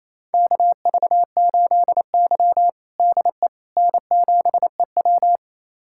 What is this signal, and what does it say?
Morse code, heard as a keyed tone, about 28 wpm, 700 Hz. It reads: KV8Y DE N7EW
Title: USA Callsign DE Callsign - 4 Characters - 28wpm